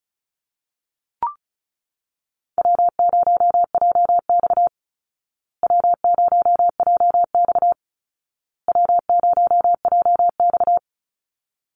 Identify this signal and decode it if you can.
Morse code, heard as a keyed tone, about 34 wpm, 700 Hz. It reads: W0JX W0JX W0JX
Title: Call Signs Repeated 2x - 35wpm